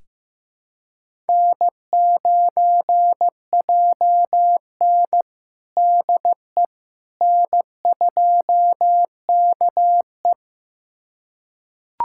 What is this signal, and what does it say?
Morse code, heard as a keyed tone, about 15 wpm, 700 Hz. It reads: N9JN DE N2KE
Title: USA Callsign DE Callsign - 4 Characters - 15wpm